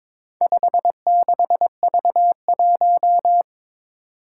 Morse code, heard as a keyed tone, about 21 wpm, 700 Hz. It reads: 56V1
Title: Four Challenging Letter-Number Combinations 22wpm